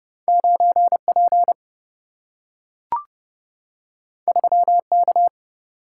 Morse code, heard as a keyed tone, about 30 wpm, 700 Hz. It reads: 9P 3K
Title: Single-Letter Single-Number 30wpm